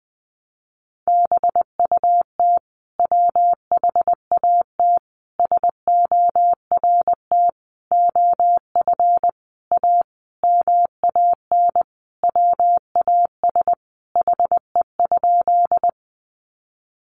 Morse code, heard as a keyed tone, about 20 wpm, 700 Hz. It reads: BUT WHAT SORT OF A MAN WAS HE?
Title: Sentences from Top 1300 Words 20wpm